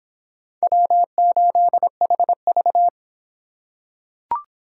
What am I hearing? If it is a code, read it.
Morse code, heard as a keyed tone, about 25 wpm, 700 Hz. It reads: W8HV